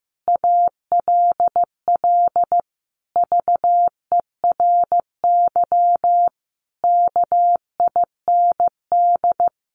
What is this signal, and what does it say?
Morse code, heard as a keyed tone, about 14 wpm, 700 Hz. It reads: ALL VERY KIND
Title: Sentences from Top 200 Words 15wpm